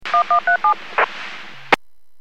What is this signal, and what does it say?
DTMF, heard as telephone touch tones, tones of about 110 ms. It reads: 1137